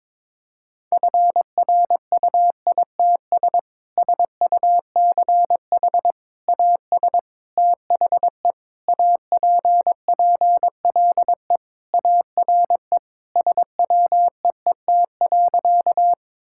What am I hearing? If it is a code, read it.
Morse code, heard as a keyed tone, about 22 wpm, 700 Hz. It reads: FRUITS SUCH AS THE APPLE ARE SWEET.
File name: practice / 22wpm / Sentences with 3-Word Phrases - 22wpm